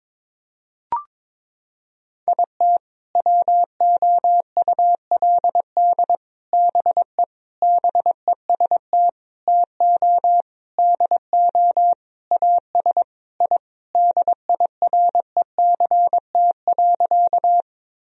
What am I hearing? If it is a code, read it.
Morse code, heard as a keyed tone, about 22 wpm, 700 Hz. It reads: IT WOULD BE BEST TO DO AS I DIRECT.
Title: Sentences from Top 1000 Words - Review Part 2 - 22wpm